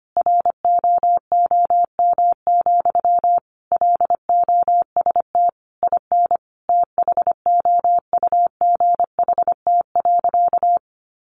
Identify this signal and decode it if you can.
Morse code, heard as a keyed tone, about 25 wpm, 700 Hz. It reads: ROOM, LOST IN THOUGHT.
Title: Sentences from Top 1400 Words 25wpm